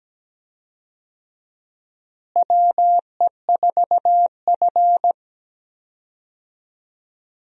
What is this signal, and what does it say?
Morse code, heard as a keyed tone, about 17 wpm, 700 Hz. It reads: WE4F
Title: USA Call Signs - 4 Characters - Repeated 3x - 17wpm